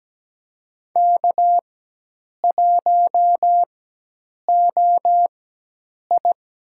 Morse code, heard as a keyed tone, about 17 wpm, spaced as for 8 wpm, 700 Hz. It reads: K1OI
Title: USA Call Signs - 4 Characters - 4x Character Spacing - 17wpm